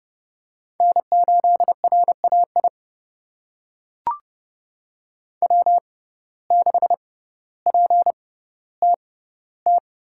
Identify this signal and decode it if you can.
Morse code, heard as a keyed tone, about 30 wpm, spaced as for 15 wpm, 700 Hz. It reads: N8RAI W6PTT
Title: USA Call Signs - 5 Characters - 6x Character Spacing - 30wpm